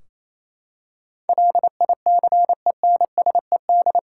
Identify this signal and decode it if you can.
Morse code, heard as a keyed tone, about 28 wpm, 700 Hz. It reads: LICENSED
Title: Practice QSO - Exchange 3 - More Info - 28wpm